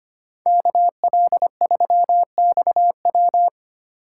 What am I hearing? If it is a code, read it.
Morse code, heard as a keyed tone, about 30 wpm, 700 Hz. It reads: KL3XW
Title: Speed Racing - USA Call Signs 25wpm